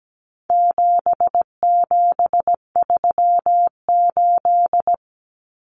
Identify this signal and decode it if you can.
Morse code, heard as a keyed tone, about 17 wpm, 700 Hz. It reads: 7738
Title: Numbers - 4-Digit 17wpm